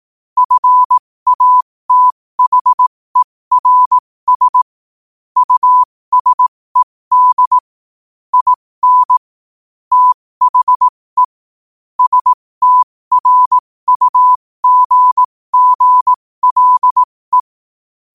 Morse code, heard as a keyed tone, about 18 wpm, spaced as for 15 wpm, 1000 Hz. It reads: FATHERS USED IN THE STRUGGLE
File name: qotdfiles / 15wpm